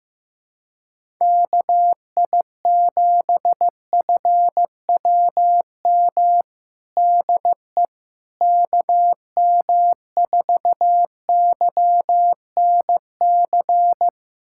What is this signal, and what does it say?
Morse code, heard as a keyed tone, about 15 wpm, 700 Hz. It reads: KI7FWM DE KM4YNC
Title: USA Callsign DE Callsign - 6 Characters - 15wpm